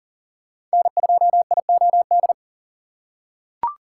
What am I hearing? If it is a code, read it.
Morse code, heard as a keyed tone, about 40 wpm, 700 Hz. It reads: N2IOD
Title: USA Call Signs - 5 Characters - 6x Character Spacing - 40wpm